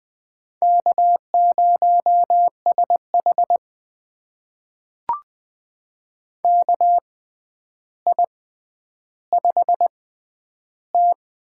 Morse code, heard as a keyed tone, about 20 wpm, spaced as for 10 wpm, 700 Hz. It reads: K0SH KI5T
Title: USA Call Signs - 4 Characters - 6x Character Spacing - 20wpm